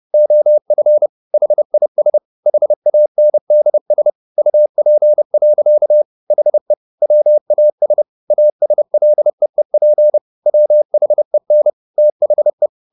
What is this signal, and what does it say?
Morse code, heard as a keyed tone, about 30 wpm, 600 Hz. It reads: OF HIS HANDS UP. HE WAS ASLEEP WHEN THE